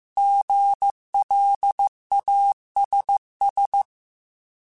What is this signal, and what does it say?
Morse code, heard as a keyed tone, about 15 wpm, 800 Hz. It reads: GLASS
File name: Alphabetical / Alphabetical mp3s / Gs @13